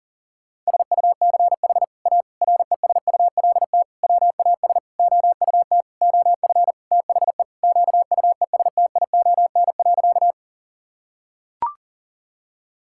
Morse code, heard as a keyed tone, about 40 wpm, 700 Hz. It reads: SUCH A RESULT WAS OUT OF THE QUESTION.
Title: Sentences from Top 1000 Words - Review Part 5 - 40wpm